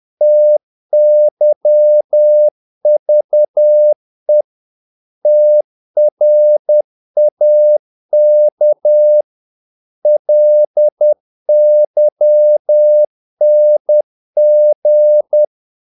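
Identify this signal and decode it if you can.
Morse code, heard as a keyed tone, about 10 wpm, 600 Hz. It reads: TYVE TRAK LYNG
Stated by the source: Morse practice of short Danish words at 10 wpm, from CW med Gnister.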